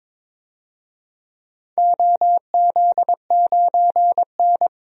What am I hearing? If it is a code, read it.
Morse code, heard as a keyed tone, about 22 wpm, 700 Hz. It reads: OZ9N
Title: DXCC-100 Call Signs - 4 Characters - 22wpm